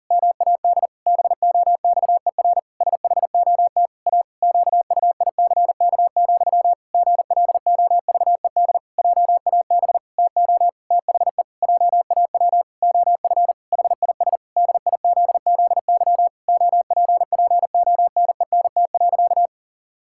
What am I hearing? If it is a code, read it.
Morse code, heard as a keyed tone, about 40 wpm, 700 Hz. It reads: MAD BOXER SHOT A QUICK, GLOVED JAB TO THE JAW OF HIS DIZZY OPPONENT.
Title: Pangram Challenge v2 - 40wpm